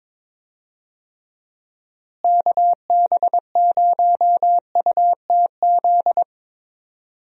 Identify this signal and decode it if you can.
Morse code, heard as a keyed tone, about 22 wpm, 700 Hz. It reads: KB0UTZ
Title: USA Call Signs - 6 Characters - Repeated 3x - 22wpm